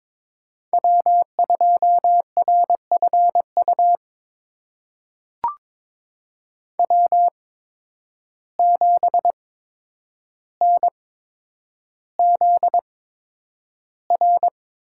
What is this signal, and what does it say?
Morse code, heard as a keyed tone, about 22 wpm, spaced as for 9 wpm, 700 Hz. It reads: W2RFU W7NZR
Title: USA Call Signs - 5 Characters - 8x Character Spacing - 22wpm